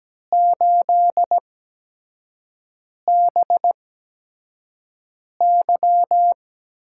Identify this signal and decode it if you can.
Morse code, heard as a keyed tone, about 17 wpm, spaced as for 5 wpm, 700 Hz. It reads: 8BY
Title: USA Call Signs - 8x Character Spacing - 17wpm